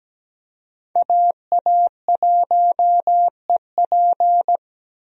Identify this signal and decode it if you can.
Morse code, heard as a keyed tone, about 17 wpm, 700 Hz. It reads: AA1EP
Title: USA Call Signs - 5 Characters - Repeated 2x - 17wpm